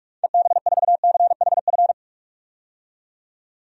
Morse code, heard as a keyed tone, about 46 wpm, 700 Hz. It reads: EB4CHF
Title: DXCC-100 Call Signs - 6 Characters - 45wpm